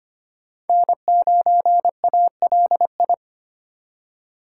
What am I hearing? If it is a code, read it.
Morse code, heard as a keyed tone, about 25 wpm, 700 Hz. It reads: N9ALI
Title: USA Call Signs - 5 Characters - 4x Character Spacing - 25wpm